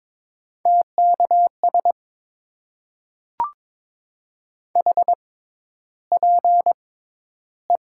Morse code, heard as a keyed tone, about 22 wpm, spaced as for 11 wpm, 700 Hz. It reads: TKS HPE
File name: Top 90 QSO Elements - 6x Character Spacing - 22wpm